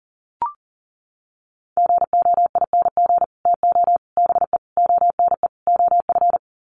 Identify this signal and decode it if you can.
Morse code, heard as a keyed tone, about 40 wpm, 700 Hz. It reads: GOING TO BE ONE OF
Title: Sets of 5 Words Encore2 40wpm